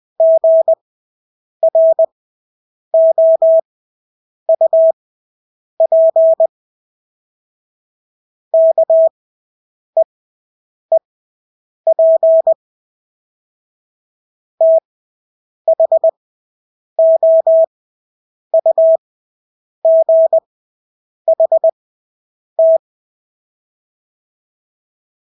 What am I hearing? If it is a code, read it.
Morse code, heard as a keyed tone, about 20 wpm, spaced as for 8 wpm, 650 Hz. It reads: GROUP KEEP THOUGHT
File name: Engelske ord 20/8 wpm